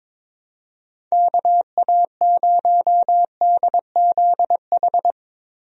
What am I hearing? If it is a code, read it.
Morse code, heard as a keyed tone, about 22 wpm, 700 Hz. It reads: KA0DZH
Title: USA Call Signs - 6 Characters - Repeated 3x - 22wpm